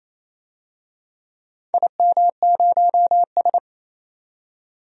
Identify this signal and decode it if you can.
Morse code, heard as a keyed tone, about 28 wpm, 700 Hz. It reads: IM0S
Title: DXCC-100 Call Signs - 4 Characters - 28wpm